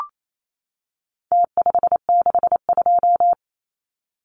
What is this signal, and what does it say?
Morse code, heard as a keyed tone, about 28 wpm, 700 Hz. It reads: T562
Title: Numbers - Contest Style - 4-Digit 28wpm